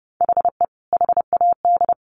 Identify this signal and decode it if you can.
Morse code, heard as a keyed tone, about 30 wpm, 700 Hz. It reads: HE HAD
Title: Sentences from Top 1100 Words 30wpm